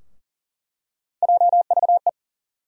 Morse code, heard as a keyed tone, about 40 wpm, 700 Hz. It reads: JVE
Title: USA Call Sign Postfixes - 3 Letters 40wpm